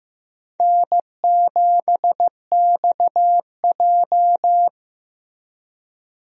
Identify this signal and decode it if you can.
Morse code, heard as a keyed tone, about 16 wpm, 700 Hz. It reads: N7XJ
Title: USA Call Signs - 4 Characters - Repeated 3x - 15wpm